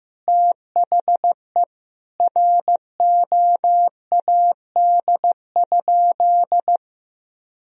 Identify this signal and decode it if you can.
Morse code, heard as a keyed tone, about 15 wpm, 700 Hz. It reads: THE ROAD?
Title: Sentences from Top 1000 Words - Review Part 2 - 15wpm